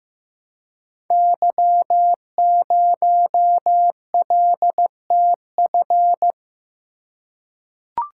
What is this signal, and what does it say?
Morse code, heard as a keyed tone, about 15 wpm, 700 Hz. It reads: Y0LTF
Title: Code Groups - 5 Letter:Numbers - 15wpm